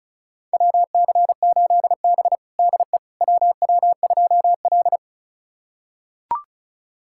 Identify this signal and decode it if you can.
Morse code, heard as a keyed tone, about 36 wpm, 700 Hz. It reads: WC8B DE WW2L
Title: USA Callsign DE Callsign - 4 Characters - 35wpm